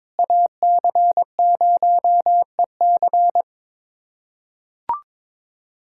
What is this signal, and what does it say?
Morse code, heard as a keyed tone, about 22 wpm, 700 Hz. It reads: AC0EC
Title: USA Callsign DE Callsign - 5 Characters - 22wpm